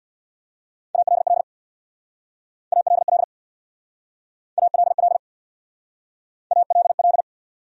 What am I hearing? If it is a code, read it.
Morse code, heard as a keyed tone, about 62 wpm, 700 Hz. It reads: ALL ALL ALL ALL
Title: Speed-Racing Top 100 Words 50wpm